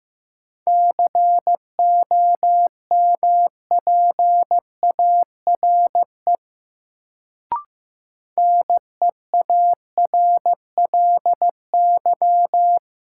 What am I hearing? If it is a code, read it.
Morse code, heard as a keyed tone, about 15 wpm, 700 Hz. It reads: COMPARE NEARLY
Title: Sentences from Top 1300 Words - New Word Drill - 15wpm